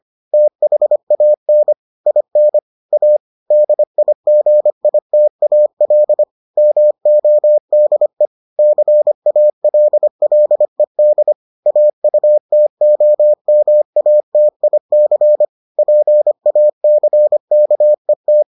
Morse code, heard as a keyed tone, about 25 wpm, 600 Hz. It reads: THAN IN A DIGITAL MODE CALLED AUTOMATIC PACKET